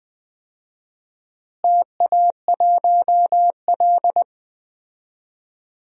Morse code, heard as a keyed tone, about 21 wpm, 700 Hz. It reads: TA1L
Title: DXCC-100 Call Signs - 4 Characters - Repeated 2x - 20wpm